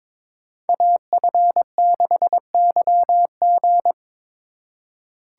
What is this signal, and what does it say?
Morse code, heard as a keyed tone, about 22 wpm, 700 Hz. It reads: AF6YG
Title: USA Call Signs - 5 Characters - 4x Character Spacing - 22wpm